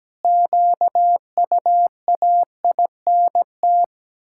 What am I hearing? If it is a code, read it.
Morse code, heard as a keyed tone, about 17 wpm, 700 Hz. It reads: QUAINT
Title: Pangram Challenge v2 - 17wpm